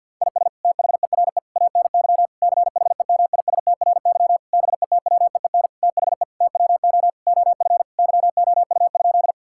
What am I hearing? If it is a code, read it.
Morse code, heard as a keyed tone, about 52 wpm, 700 Hz. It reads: IS THERE ANY CHEMISTRY BETWEEN THE TWO OF YOU?
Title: Sentences from Top 1000 Words - Review Part 6 - 50wpm